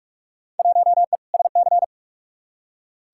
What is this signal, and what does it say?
Morse code, heard as a keyed tone, about 43 wpm, 700 Hz. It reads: 1E SC
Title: Field Day Exchanges - 45wpm